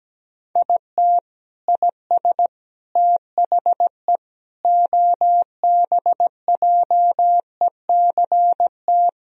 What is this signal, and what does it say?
Morse code, heard as a keyed tone, about 17 wpm, 700 Hz. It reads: IT IS THE OBJECT
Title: Sentences from Top 1000 Words - Review Part 4 - 17wpm